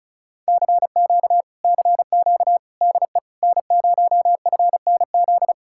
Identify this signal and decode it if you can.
Morse code, heard as a keyed tone, about 40 wpm, 700 Hz. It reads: CQ CQ DE N0FNZ